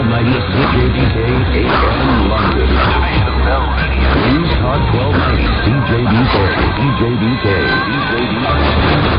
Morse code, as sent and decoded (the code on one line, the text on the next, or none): none